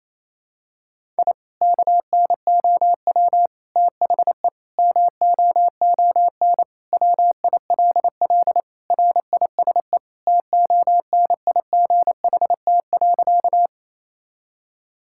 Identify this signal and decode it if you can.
Morse code, heard as a keyed tone, about 28 wpm, 700 Hz. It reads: I KNOW THE MOON WILL RISE TONIGHT.
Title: Sentences from Top 1000 Words - Review Part 6 - 28wpm